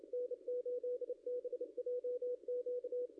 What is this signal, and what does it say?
Morse code, heard as a keyed tone, about 29 wpm, 500 Hz, with radio noise. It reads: N8BJQ
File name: contests / ss-cw-k5tr-op-n5rz -0600